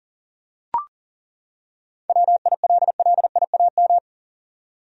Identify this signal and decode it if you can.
Morse code, heard as a keyed tone, about 37 wpm, 700 Hz. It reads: WILLIAM
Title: USA Common Names - 40wpm